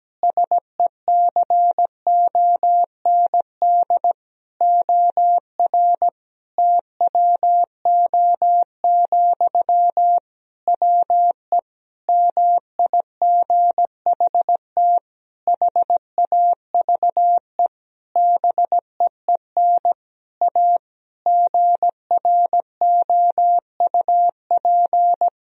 Morse code, heard as a keyed tone, about 17 wpm, 700 Hz. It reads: SECOND OR TWO, WE MIGHT HAVE BEEN A GROUP
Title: Sentences from Top 1000 Words - Review Part 1 - 17wpm